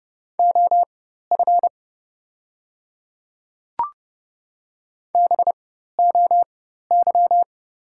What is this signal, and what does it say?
Morse code, heard as a keyed tone, about 30 wpm, spaced as for 14 wpm, 700 Hz. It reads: OF BOY
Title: 4x Character Spacing - Top 200 Words - 30wpm